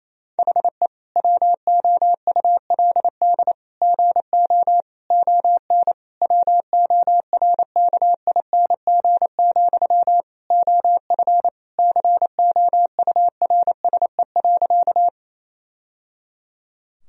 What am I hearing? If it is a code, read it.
Morse code, heard as a keyed tone, about 27 wpm, 700 Hz. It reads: HE WOULD GO ON WORKING, OF COURSE.
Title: Sentences from Top 1000 Words - Review Part 4 - 28wpm